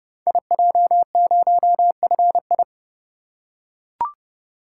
Morse code, heard as a keyed tone, about 30 wpm, 700 Hz. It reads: IJ0FI
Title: Code Groups - 5 Letter:Numbers - 30wpm